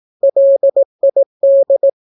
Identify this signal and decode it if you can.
Morse code, heard as a keyed tone, about 18 wpm, 550 Hz. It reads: LID
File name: naukapodstaw / skroty / skroty